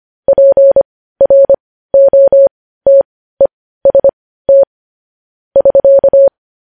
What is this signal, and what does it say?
Morse code, heard as a keyed tone, about 25 wpm, spaced as for 15 wpm, 550 Hz. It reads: PROTEST <SK>
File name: podcast / News headlines-1x-25x15